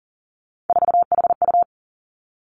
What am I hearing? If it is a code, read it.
Morse code, heard as a keyed tone, about 38 wpm, 700 Hz. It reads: 4HU
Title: Three Challenging Letter-Number Combinations 40wpm